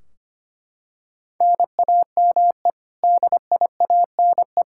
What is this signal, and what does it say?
Morse code, heard as a keyed tone, about 25 wpm, 700 Hz. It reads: NAME DIANE
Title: Practice QSO - Exchange 1 - Beginning - 25wpm